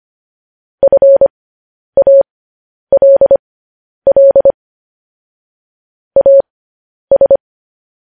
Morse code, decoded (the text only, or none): FALL AS